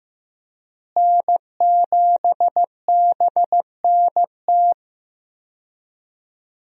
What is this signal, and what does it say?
Morse code, heard as a keyed tone, about 15 wpm, 700 Hz. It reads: N7BNT
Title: USA Call Signs - 5 Characters - Repeated 2x - 15wpm